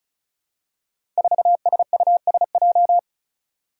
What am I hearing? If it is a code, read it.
Morse code, heard as a keyed tone, about 36 wpm, 700 Hz. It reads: 4SUSJ
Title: Five Challenging Letter-Number Combinations 35wpm